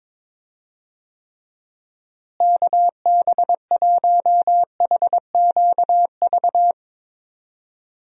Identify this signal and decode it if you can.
Morse code, heard as a keyed tone, about 22 wpm, 700 Hz. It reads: KB1HQV
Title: USA Call Signs - 6 Characters - Repeated 2x - 22wpm